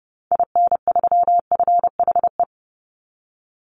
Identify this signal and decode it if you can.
Morse code, heard as a keyed tone, about 30 wpm, 700 Hz. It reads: IN3FHE
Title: Call Signs Repeated 2x - 30wpm